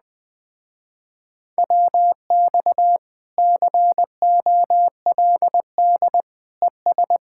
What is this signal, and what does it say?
Morse code, heard as a keyed tone, about 20 wpm, 700 Hz. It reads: WX COLD ES
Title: Speed-Racing Common QSO Phrases Encore - 17wpm